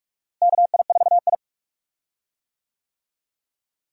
Morse code, heard as a keyed tone, about 42 wpm, 700 Hz. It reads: KI4I
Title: USA Call Signs - 4 Characters - Repeated 2x - 45wpm